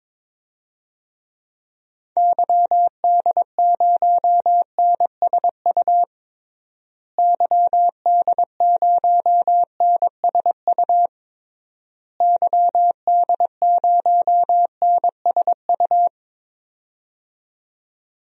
Morse code, heard as a keyed tone, about 22 wpm, 700 Hz. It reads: YD0NSU YD0NSU YD0NSU
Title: DXCC-100 Call Signs - 6 Characters - Repeated 3x - 22wpm